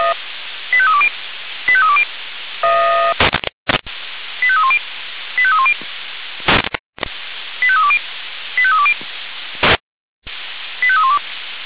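Five-tone selective call (ZVEI), heard as quick runs of five tones, tones of about 70 ms. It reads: 85320 85320 85310 85310 85320 85320 85312
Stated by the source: ZVEI five-tone call 85320 received by SDR.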